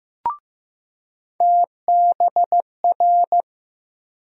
Morse code, heard as a keyed tone, about 15 wpm, 700 Hz. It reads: TBR
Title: USA Call Sign Postfixes - 3 Letters 15wpm